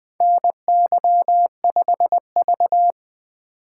Morse code, heard as a keyed tone, about 20 wpm, 700 Hz. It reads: NY5V
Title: USA Call Signs - 4 Characters - 2x Character Spacing - 20wpm